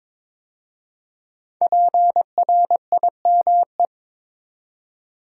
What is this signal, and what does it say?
Morse code, heard as a keyed tone, about 22 wpm, 700 Hz. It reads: PRIME
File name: 5 Letter Words - First 3 Letters Given - 22wpm